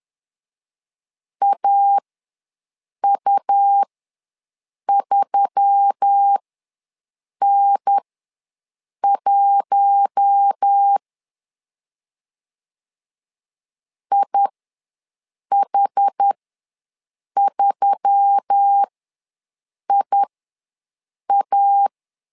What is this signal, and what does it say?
Morse code, heard as a keyed tone, about 10 wpm, spaced as for 6 wpm, 800 Hz. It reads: AU3N1 IH3IA
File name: LEZIONI / 4 Q1S3 ws 10wpm 2437